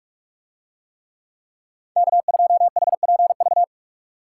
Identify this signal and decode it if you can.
Morse code, heard as a keyed tone, about 44 wpm, 700 Hz. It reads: K2HPV